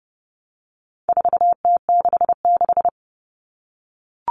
Morse code, heard as a keyed tone, about 30 wpm, 700 Hz. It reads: 4T66
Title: All Cut Numbers - 4-Digit 30wpm